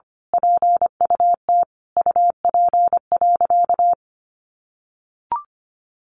Morse code, heard as a keyed tone, about 25 wpm, 700 Hz. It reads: PUT UP.
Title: A Sound Of Thunder 25wpm